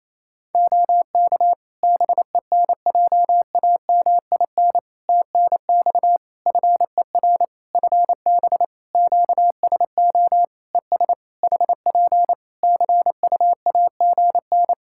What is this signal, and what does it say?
Morse code, heard as a keyed tone, about 28 wpm, 700 Hz. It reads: OK BENJAMIN TNX FER FB QSO ES HP CUAGN
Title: K1USN SST QSO Practice 28wpm